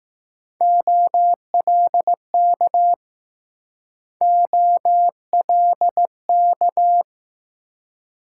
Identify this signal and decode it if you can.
Morse code, heard as a keyed tone, about 20 wpm, 700 Hz. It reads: OLK OLK